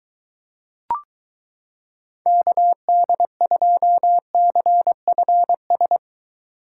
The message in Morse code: KD2CFS

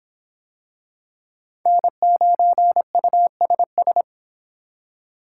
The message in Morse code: N9USS

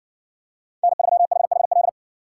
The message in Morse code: N3HHL

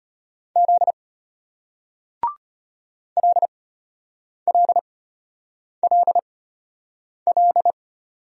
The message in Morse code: Z LLLL

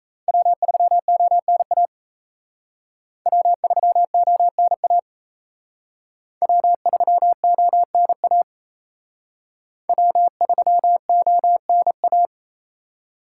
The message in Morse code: W3ONA W3ONA W3ONA W3ONA

